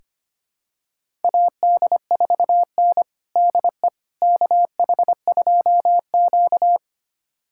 Morse code, heard as a keyed tone, about 25 wpm, 700 Hz. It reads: AD4N DE KH2Q
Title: USA Callsign DE Callsign - 4 Characters - 25wpm